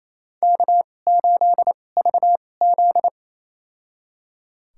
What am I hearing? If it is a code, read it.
Morse code, heard as a keyed tone, about 28 wpm, spaced as for 20 wpm, 700 Hz. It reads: K8VZ
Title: USA Call Signs - 2x Character Spacing - 28wpm